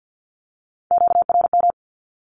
Morse code, heard as a keyed tone, about 50 wpm, 700 Hz. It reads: QRM
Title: Q-Codes with Spoken Meaning - 50wpm